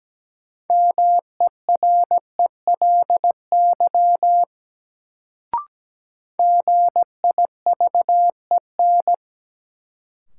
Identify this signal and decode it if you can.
Morse code, heard as a keyed tone, about 17 wpm, 700 Hz. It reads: MERELY GIVEN
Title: Sentences from Top 1400 Words - New Word Drill Alt - 17wpm